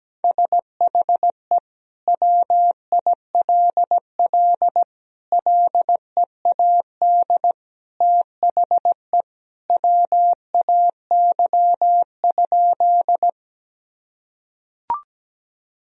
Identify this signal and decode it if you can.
Morse code, heard as a keyed tone, about 17 wpm, 700 Hz. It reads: SHE WILL LEAD THE WAY?
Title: Sentences from Top 1000 Words - Review Part 2 - 17wpm